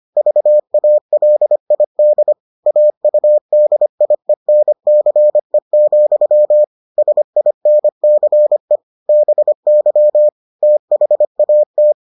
Morse code, heard as a keyed tone, about 25 wpm, 600 Hz. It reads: VALID AUDIENCE, SINCE BY THAT